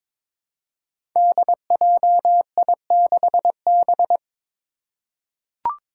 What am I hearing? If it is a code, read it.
Morse code, heard as a keyed tone, about 22 wpm, 700 Hz. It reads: DJI6B
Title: Five Challenging Letter-Number Combinations 22wpm